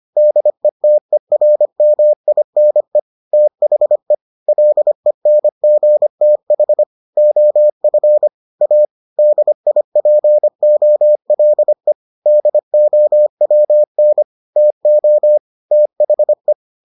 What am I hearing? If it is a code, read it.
Morse code, heard as a keyed tone, about 25 wpm, 600 Hz. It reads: DETERMINE THE LENGTH OF A DIPOLE DOWN TO THE